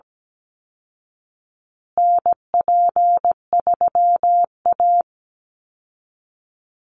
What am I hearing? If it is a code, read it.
Morse code, heard as a keyed tone, about 17 wpm, 700 Hz. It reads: NP3A
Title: Call Signs Repeated 3x - 17wpm